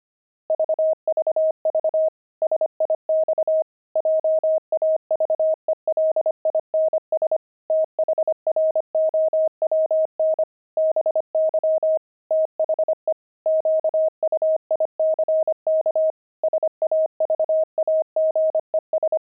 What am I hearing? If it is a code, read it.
Morse code, heard as a keyed tone, about 25 wpm, 630 Hz. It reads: VVV SIX JAVELINS THROWN BY THE QUICK SAVAGES
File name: pangram-fs-25-cs-25-javelins